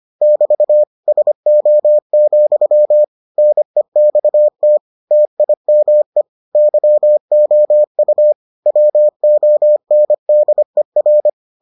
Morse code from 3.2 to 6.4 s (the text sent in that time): NEXT TIME